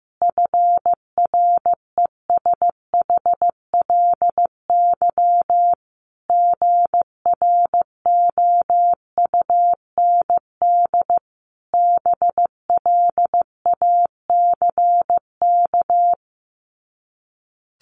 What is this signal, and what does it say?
Morse code, heard as a keyed tone, about 15 wpm, 700 Hz. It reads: FRESHLY GROUND BLACK
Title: Sets of 5 Words Encore 15wpm